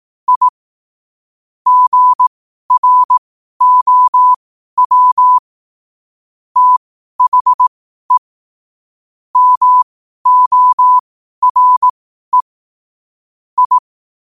Morse code, heard as a keyed tone, about 18 wpm, spaced as for 12 wpm, 1000 Hz. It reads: I GROW THE MORE I
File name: qotdfiles / 12wpm